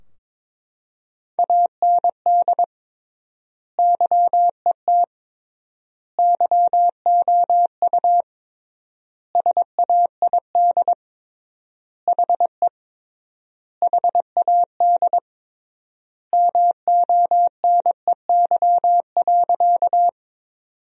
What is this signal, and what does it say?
Morse code, heard as a keyed tone, about 22 wpm, 700 Hz. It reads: AND YET YOU SAID HE HAD MONEY.
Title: 3x Word-Spacing - Sentences from Top 500 Words - 22wpm